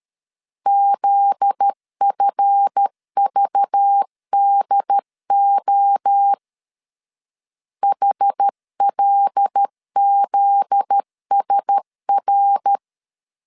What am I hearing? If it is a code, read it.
Morse code, heard as a keyed tone, about 13 wpm, 800 Hz. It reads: ZFVDO HLZSR
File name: LEZIONI / 7 GF9Z / 7 GF9Z 12wpm 2147